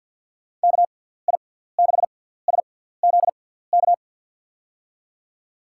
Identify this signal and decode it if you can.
Morse code, heard as a keyed tone, about 50 wpm, spaced as for 17 wpm, 700 Hz. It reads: KI6SZK
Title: USA Call Signs - 6 Characters - 6x Character Spacing - 50wpm